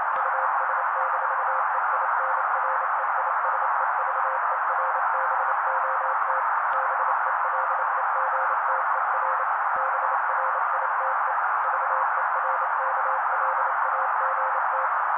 Morse code, heard as a keyed tone, about 28 wpm, 530 Hz. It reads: US-EINREISEVERBOT BELEGTER BRITE VERKLAGT
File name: daily / 27wpm noise